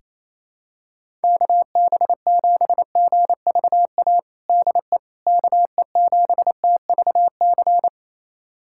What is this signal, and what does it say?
Morse code, heard as a keyed tone, about 28 wpm, 700 Hz. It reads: KB7GVA DE KE7TVC
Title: USA Callsign DE Callsign - 6 Characters - 28wpm